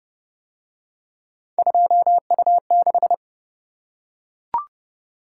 Morse code, heard as a keyed tone, about 29 wpm, 700 Hz. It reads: 2U6